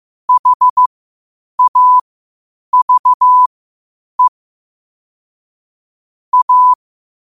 Morse code, heard as a keyed tone, about 15 wpm, spaced as for 8 wpm, 1000 Hz. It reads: HAVE A
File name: qotdfiles / 8wpm